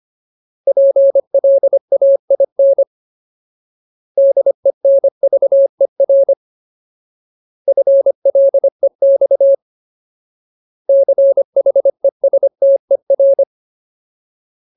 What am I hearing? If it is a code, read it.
Morse code, heard as a keyed tone, about 25 wpm, 550 Hz. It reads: PLAIN DENVER FLEX CHESTER
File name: Ultimate Word List / 25WPM Ultimate